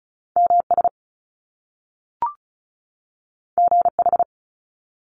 Morse code, heard as a keyed tone, about 36 wpm, 700 Hz. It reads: MS GH
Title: Top Two Letter Combinations in Words 35wpm